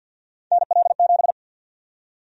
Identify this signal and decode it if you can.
Morse code, heard as a keyed tone, about 51 wpm, 700 Hz. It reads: NL7
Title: USA Call Sign Prefixes - 50wpm